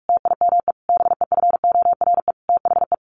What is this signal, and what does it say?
Morse code, heard as a keyed tone, about 46 wpm, 700 Hz. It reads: TIME BEFORE THE
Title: A Sound Of Thunder 45wpm